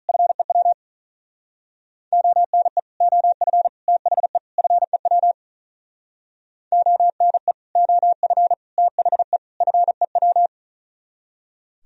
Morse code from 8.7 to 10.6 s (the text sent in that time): THE FEW